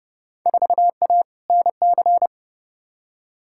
Morse code, heard as a keyed tone, about 30 wpm, 700 Hz. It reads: 4A NC
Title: Field Day Exchanges - 30wpm